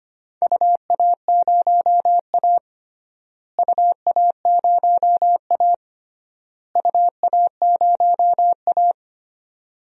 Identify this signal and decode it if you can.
Morse code, heard as a keyed tone, about 25 wpm, 700 Hz. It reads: UA0A UA0A UA0A
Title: DXCC-100 Call Signs - 4 Characters - Repeated 3x - 25wpm